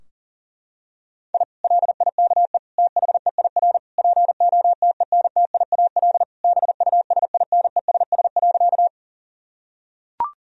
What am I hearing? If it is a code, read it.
Morse code, heard as a keyed tone, about 40 wpm, 700 Hz. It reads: I LIKE THEIR POTENTIAL BUSINESS.
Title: Sentences from Top 1000 Words - Review Part 4 - 40wpm